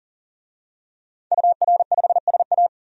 Morse code, heard as a keyed tone, about 40 wpm, 700 Hz. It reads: UR5SA
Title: DXCC-100 Call Signs - 5 Characters - 40wpm